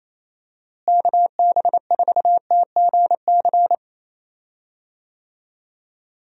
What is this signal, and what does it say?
Morse code, heard as a keyed tone, about 28 wpm, 700 Hz. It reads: KB4TGC
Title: USA Call Signs - 6 Characters - Repeated 2x - 28wpm